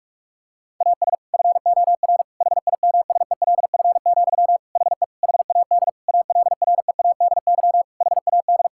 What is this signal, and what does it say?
Morse code, heard as a keyed tone, about 41 wpm, 700 Hz. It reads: AS FOR HIMSELF, HE HAD ALREADY HAD
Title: Sentences from Top 1600 Words 45wpm